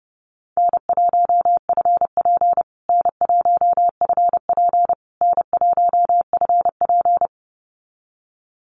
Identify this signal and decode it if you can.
Morse code, heard as a keyed tone, about 30 wpm, 700 Hz. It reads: N1FP N1FP N1FP
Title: USA Call Signs Repeated 3x 30wpm